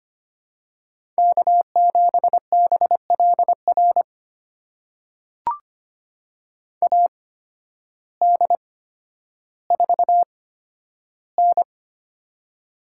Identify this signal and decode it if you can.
Morse code, heard as a keyed tone, about 25 wpm, spaced as for 12 wpm, 700 Hz. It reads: K7BLR AD4N
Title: USA Call Signs - 8x Character Spacing - 25wpm